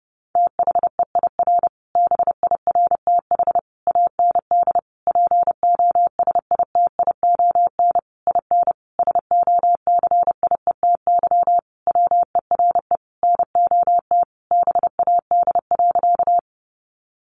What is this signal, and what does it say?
Morse code, heard as a keyed tone, about 28 wpm, 700 Hz. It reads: THEIR BIRTH AND POSITION IN SOCIETY WERE NOT BAD.
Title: Sentences from Top 900 Words 30wpm